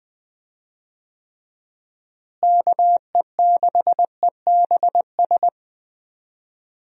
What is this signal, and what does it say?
Morse code, heard as a keyed tone, about 20 wpm, 700 Hz. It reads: KE6EBS
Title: USA Call Signs - 6 Characters - Repeated 3x - 20wpm